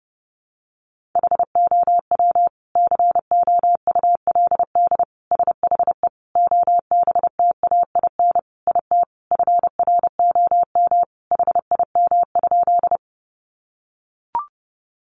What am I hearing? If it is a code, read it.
Morse code, heard as a keyed tone, about 30 wpm, 700 Hz. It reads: HOW COULD SHE OBTAIN IT FROM HIM?
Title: Sentences from Top 1300 Words 30wpm